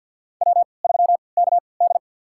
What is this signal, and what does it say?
Morse code, heard as a keyed tone, about 47 wpm, spaced as for 28 wpm, 700 Hz. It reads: W3KD
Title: USA Call Signs - 4 Characters - 3x Character Spacing - 50wpm